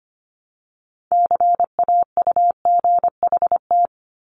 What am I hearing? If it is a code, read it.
Morse code, heard as a keyed tone, about 25 wpm, 700 Hz. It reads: CAUGHT
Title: Multi-letter Phonemes 25wpm